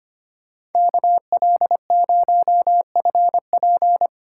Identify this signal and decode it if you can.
Morse code, heard as a keyed tone, about 32 wpm, 700 Hz. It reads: KL0FP